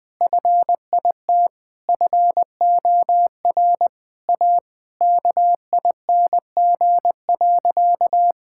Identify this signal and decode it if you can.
Morse code, heard as a keyed tone, about 20 wpm, 700 Hz. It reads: FIT FOR A KING.
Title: Sentences from Top 1000 Words - Review Part 5 - 20wpm